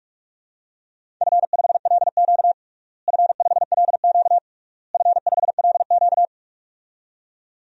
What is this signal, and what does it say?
Morse code, heard as a keyed tone, about 41 wpm, 700 Hz. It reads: F5LQ F5LQ F5LQ
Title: DXCC-100 Call Signs - 4 Characters - Repeated 3x - 45wpm